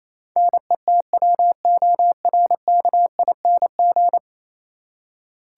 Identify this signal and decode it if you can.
Morse code, heard as a keyed tone, about 27 wpm, 700 Hz. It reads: NETWORKING
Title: Practice QSO - Exchange 3 - More Info - 28wpm